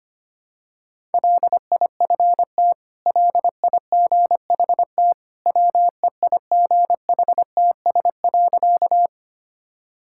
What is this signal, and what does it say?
Morse code, heard as a keyed tone, about 25 wpm, 700 Hz. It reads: LIFT LIGHT WEIGHTS.
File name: Easy Sentences with Hints - 25wpm